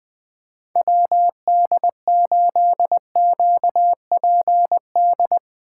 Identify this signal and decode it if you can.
Morse code, heard as a keyed tone, about 20 wpm, 700 Hz. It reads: WD8QPD
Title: Speed Racing - USA 2x3 Call Signs 17wpm